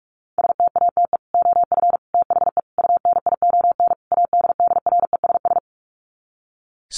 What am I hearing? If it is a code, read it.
Morse code, heard as a keyed tone, about 44 wpm, 700 Hz. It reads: STATE OF THE UNION ADDRESS
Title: Sets of 5 Words Encore 45wpm